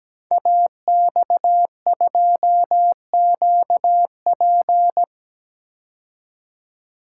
Morse code, heard as a keyed tone, about 17 wpm, 700 Hz. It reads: AX2QP
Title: Code Groups - 5 Letter:Numbers - 17wpm